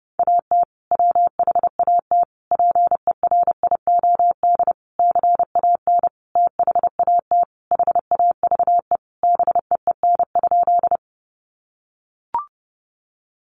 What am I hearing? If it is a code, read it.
Morse code, heard as a keyed tone, about 30 wpm, 700 Hz. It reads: AT WHAT PERIOD CAN THAT HAVE BEEN?
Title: Sentences from Top 800 Words 30wpm